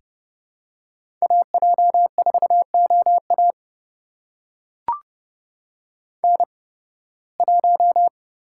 Morse code, heard as a keyed tone, about 31 wpm, spaced as for 20 wpm, 700 Hz. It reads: AJ4OA N1
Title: USA Call Signs - 5 Characters - 8x Character Spacing - 30wpm